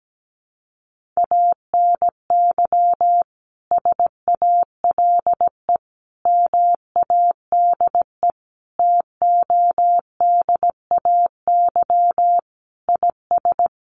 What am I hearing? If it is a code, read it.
Morse code, heard as a keyed tone, about 17 wpm, 700 Hz. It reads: ANY SALE MADE TODAY IS
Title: Sentences from Top 1400 Words 17wpm